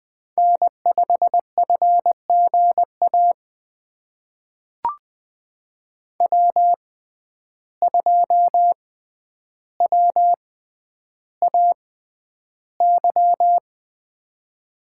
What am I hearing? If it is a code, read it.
Morse code, heard as a keyed tone, about 20 wpm, spaced as for 10 wpm, 700 Hz. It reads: N5FGA W2WAY
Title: USA Call Signs - 5 Characters - 6x Character Spacing - 20wpm